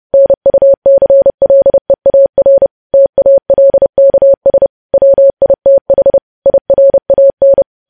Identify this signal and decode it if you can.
Morse code, heard as a keyed tone, about 30 wpm, 550 Hz. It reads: NUCLEAR TALKS WITH IRAN
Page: news headlines in morse code at 30 wpm